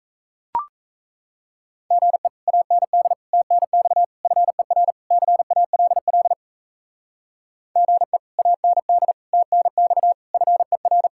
Morse code, heard as a keyed tone, about 34 wpm, 700 Hz. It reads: GE AND TNX FER CALL GE AND TNX FER
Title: Speed-Racing Common QSO Phrases Encore - 28wpm